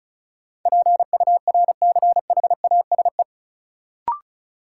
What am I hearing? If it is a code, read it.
Morse code, heard as a keyed tone, about 34 wpm, 700 Hz. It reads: PURCHASE
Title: Sentences from Top 800 Words - New Word Drill - 35wpm